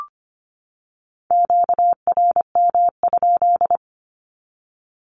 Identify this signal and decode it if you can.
Morse code, heard as a keyed tone, about 25 wpm, 700 Hz. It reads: QRM?
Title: Q-Codes 25wpm